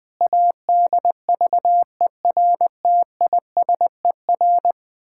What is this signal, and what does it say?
Morse code, heard as a keyed tone, about 20 wpm, 700 Hz. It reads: ADVERTISER
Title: Pangram Challenge v2 - 20wpm